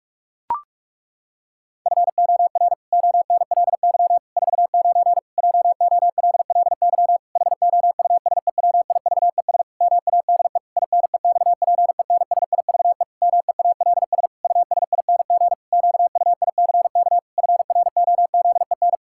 Morse code, heard as a keyed tone, about 46 wpm, 700 Hz. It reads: FOR ONLY 49 JOLLY HOUSEWIVES MADE INEXPENSIVE MEALS USING QUICK FROZEN
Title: Pangram Challenge v2 - 45wpm